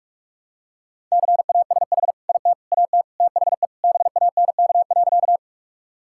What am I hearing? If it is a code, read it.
Morse code, heard as a keyed tone, about 41 wpm, 700 Hz. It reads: CASH IT AT THE BANK.
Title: Sentences from Top 1000 Words - Review Part 5 - 45wpm